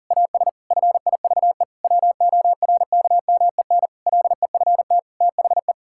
Morse code, heard as a keyed tone, about 40 wpm, 700 Hz. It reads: AS FIVE WORKMEN LEFT THE
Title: Pangram Challenge v2 - 40wpm